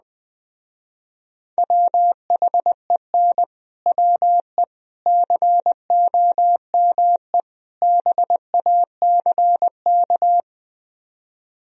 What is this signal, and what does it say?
Morse code, heard as a keyed tone, about 21 wpm, 700 Hz. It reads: WHEN WE COME BACK